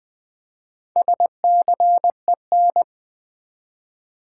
Morse code, heard as a keyed tone, about 20 wpm, 700 Hz. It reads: SCEN
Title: Build A Word - 1 Letter At A Time - 5-Letter Words - 20wpm